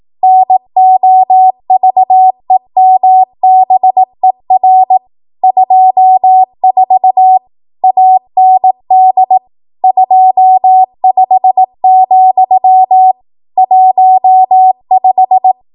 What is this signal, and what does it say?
Morse code, heard as a keyed tone, about 18 wpm, 750 Hz. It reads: NOVEMBER 24 AND 25, 15
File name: Archive / 18 WPM / 18WPM